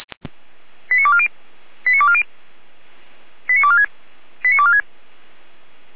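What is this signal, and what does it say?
Five-tone selective call (ZVEI), heard as quick runs of five tones, tones of about 70 ms. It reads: 89240 89240 89246 89246
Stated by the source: ZVEI five-tone call 89208 received by SDR.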